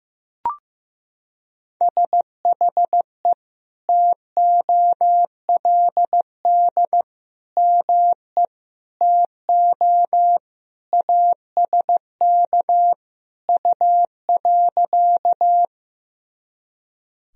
Morse code, heard as a keyed tone, about 15 wpm, 700 Hz. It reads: SHE TOLD ME TO ASK U.
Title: Sentences from Top 400 Words and Abbreviations 15wpm